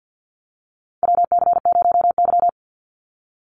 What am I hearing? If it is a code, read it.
Morse code, heard as a keyed tone, about 49 wpm, 700 Hz. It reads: WC0Y
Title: NA SOTA Chaser Callsigns 50wpm